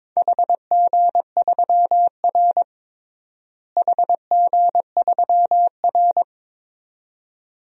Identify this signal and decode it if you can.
Morse code, heard as a keyed tone, about 22 wpm, 700 Hz. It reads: HG3R HG3R
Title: DXCC-100 Call Signs - 4 Characters - Repeated 2x - 22wpm